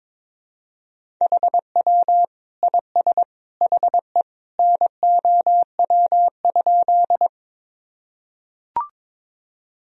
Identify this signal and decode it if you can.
Morse code, heard as a keyed tone, about 22 wpm, 700 Hz. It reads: HW IS HE NOW?
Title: Sentences from Top 100 Words and Abbreviations 22wpm